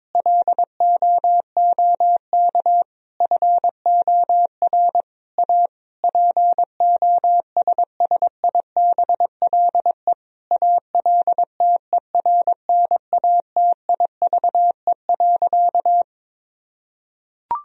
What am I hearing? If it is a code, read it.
Morse code, heard as a keyed tone, about 21 wpm, 700 Hz. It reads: LOOK FOR A POSSIBLE ALTERNATIVE.
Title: Sentences from Top 1000 Words - Review Part 4 - 22wpm